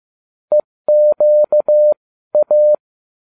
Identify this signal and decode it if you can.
Morse code, heard as a keyed tone, about 15 wpm, spaced as for 13 wpm, 600 Hz. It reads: EQA